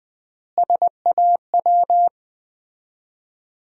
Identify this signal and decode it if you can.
Morse code, heard as a keyed tone, about 20 wpm, 700 Hz. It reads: SAW
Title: 4x Word-Spacing - Sentences from Top 500 Words - 20wpm